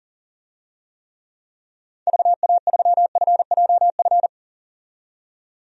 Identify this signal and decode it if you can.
Morse code, heard as a keyed tone, about 40 wpm, 700 Hz. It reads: VA3FJF
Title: DXCC-100 Call Signs - 6 Characters - 40wpm